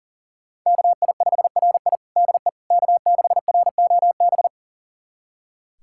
Morse code, heard as a keyed tone, about 37 wpm, 700 Hz. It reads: KI5RI DE K6ROB
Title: USA Callsign DE Callsign - 5 Characters - 40wpm